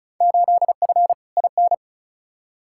Morse code, heard as a keyed tone, about 34 wpm, 700 Hz. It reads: 8F IN
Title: Field Day Exchanges Encore - 35wpm